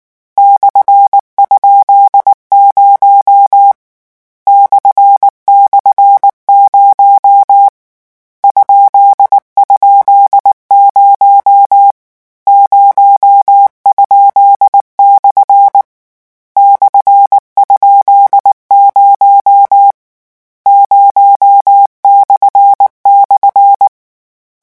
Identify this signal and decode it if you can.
Morse code, heard as a keyed tone, about 19 wpm, 800 Hz. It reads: /?0 //0 ??0 0?/ /?0 0//
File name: morse mp3 / lecon13-vitesse 18